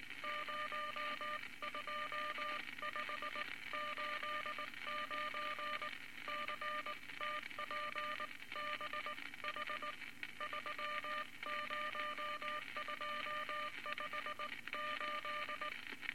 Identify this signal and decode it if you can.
Morse code, heard as a keyed tone, about 20 wpm, 1200 Hz, with noise. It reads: 02589 CTPBH 30258
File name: Morse Code Records / 19 B LCM 167 Twelve Words Per Minute